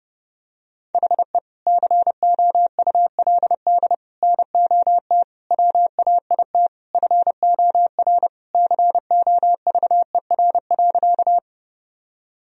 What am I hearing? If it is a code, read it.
Morse code, heard as a keyed tone, about 30 wpm, 700 Hz. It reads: HE COULD NOT WAIT FOR COVER.
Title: Sentences from Top 1000 Words - Review Part 1 - 30wpm